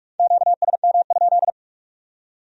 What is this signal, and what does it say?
Morse code, heard as a keyed tone, about 56 wpm, 700 Hz. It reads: QSM?
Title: Speed-Racing Q-Codes - 45wpm